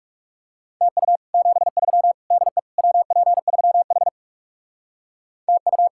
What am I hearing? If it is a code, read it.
Morse code, heard as a keyed tone, about 37 wpm, 700 Hz. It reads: TU 73 DE WP3H TU
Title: Speed-Racing Common QSO Phrases - 30wpm